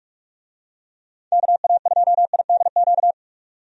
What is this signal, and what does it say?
Morse code, heard as a keyed tone, about 44 wpm, 700 Hz. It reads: KA2IDQ
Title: USA Call Signs - 6 Characters - 45wpm